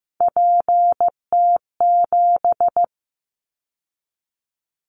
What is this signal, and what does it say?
Morse code, heard as a keyed tone, about 15 wpm, 700 Hz. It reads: PT7
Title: Callsign Prefixes for Top 50 DXCC - 15wpm